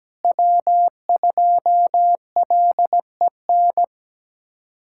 Morse code, heard as a keyed tone, about 17 wpm, 700 Hz. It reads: W2LEN
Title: USA Callsign DE Callsign 17wpm